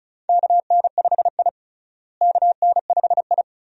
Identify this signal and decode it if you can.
Morse code, heard as a keyed tone, about 34 wpm, 700 Hz. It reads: KN5I KN5I
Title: USA Call Signs - 4 Characters - Repeated 2x - 35wpm